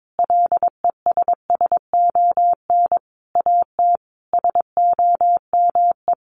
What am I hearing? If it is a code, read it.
Morse code, heard as a keyed tone, about 22 wpm, 700 Hz. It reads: LESSON AT SOME
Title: Sentences from Top 1200 Words 22wpm